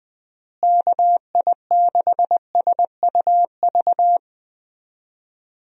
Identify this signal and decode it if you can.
Morse code, heard as a keyed tone, about 20 wpm, 700 Hz. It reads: KI6SUV
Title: USA Call Signs - 6 Characters - Repeated 2x - 20wpm